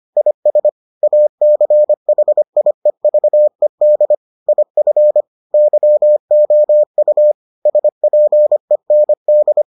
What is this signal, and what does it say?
Morse code, heard as a keyed tone, about 25 wpm, 600 Hz. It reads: IS ACHIEVED IF YOU SPEND